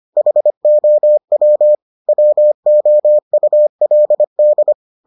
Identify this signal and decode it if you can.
Morse code, heard as a keyed tone, about 25 wpm, 600 Hz. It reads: HOW WOULD